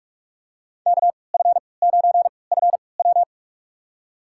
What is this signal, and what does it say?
Morse code, heard as a keyed tone, about 42 wpm, spaced as for 25 wpm, 700 Hz. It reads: KF9FW
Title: USA Call Signs - 5 Characters - 3x Character Spacing - 45wpm